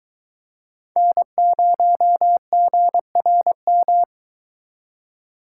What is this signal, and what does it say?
Morse code, heard as a keyed tone, about 21 wpm, 700 Hz. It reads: N0GRM